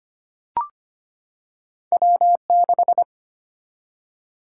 Morse code, heard as a keyed tone, about 25 wpm, 700 Hz. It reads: W6
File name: Two Letter-Number Combinations - 25wpm